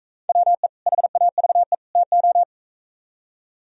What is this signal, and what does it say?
Morse code, heard as a keyed tone, about 42 wpm, 700 Hz. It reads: WE HAVE TO